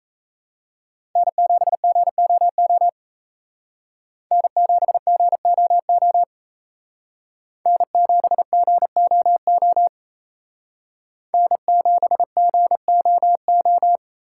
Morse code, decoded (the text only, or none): N7GOO N7GOO N7GOO N7GOO